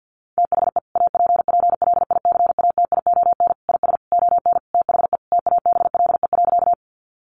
Morse code, heard as a keyed tone, about 50 wpm, 700 Hz. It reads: THE APPLICATION IS ON THE TABLE.
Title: Sentences from Top 1200 Words 50wpm